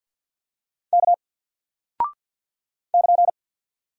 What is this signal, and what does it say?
Morse code, heard as a keyed tone, about 48 wpm, 700 Hz. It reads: K <KN>
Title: Single Character (Letter:Number:Punctuation:Prosign) Alt 50wpm